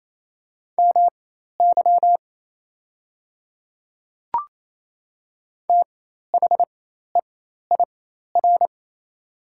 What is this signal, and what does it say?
Morse code, heard as a keyed tone, about 28 wpm, spaced as for 13 wpm, 700 Hz. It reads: MY THEIR
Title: 4x Character Spacing - Top 100 Words - 28wpm